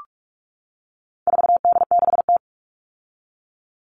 Morse code, heard as a keyed tone, about 44 wpm, 700 Hz. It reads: 4D6T
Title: All Cut Numbers - 4-Digit 45wpm